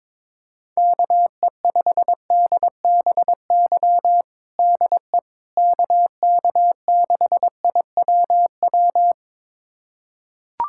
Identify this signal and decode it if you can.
Morse code, heard as a keyed tone, about 22 wpm, 700 Hz. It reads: KE5DBY DE KK6IWW
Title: USA Callsign DE Callsign - 6 Characters - 22wpm